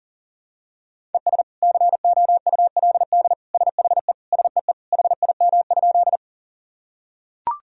Morse code, heard as a keyed tone, about 40 wpm, 700 Hz. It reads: ES COULD SHE SEE HIM?
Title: Sentences from Top 100 Words and Abbreviations 40wpm